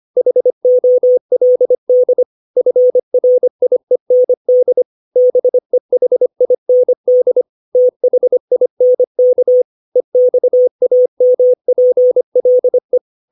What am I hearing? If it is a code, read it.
Morse code, heard as a keyed tone, about 25 wpm, 500 Hz. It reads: HOLD FRIEND BEHIND THINK EXAMPLE
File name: 632 words / 632 Words 25WPM